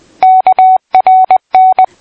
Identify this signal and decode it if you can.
Morse code, heard as a keyed tone, about 20 wpm, 750 Hz. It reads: KRN
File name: PI4AA Uitzending